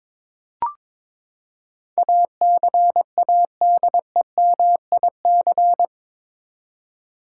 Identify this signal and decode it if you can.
Morse code, heard as a keyed tone, about 21 wpm, 700 Hz. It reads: ACADEMIC
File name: practice / 22wpm / Eight Letter Words - 22wpm